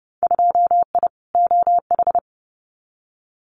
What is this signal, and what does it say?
Morse code, heard as a keyed tone, about 30 wpm, 700 Hz. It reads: 2I OH
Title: Winter Field Day Exchanges 30wpm